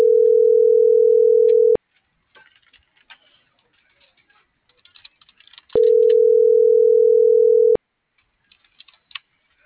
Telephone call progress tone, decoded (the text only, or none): ringback tone